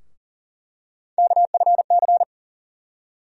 Morse code, heard as a keyed tone, about 40 wpm, 700 Hz. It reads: KFC
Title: USA Call Sign Postfixes - 3 Letters 40wpm